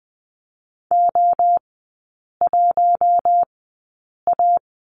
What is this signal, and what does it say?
Morse code, heard as a keyed tone, about 20 wpm, spaced as for 8 wpm, 700 Hz. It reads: O1A
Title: 067 Lesson 12e - Letter d - Three Characters 20wpm